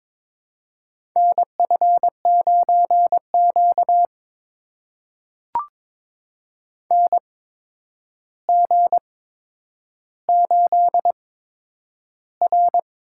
Morse code, heard as a keyed tone, about 21 wpm, spaced as for 9 wpm, 700 Hz. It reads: NF9Q NG8R
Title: USA Call Signs - 4 Characters - 8x Character Spacing - 22wpm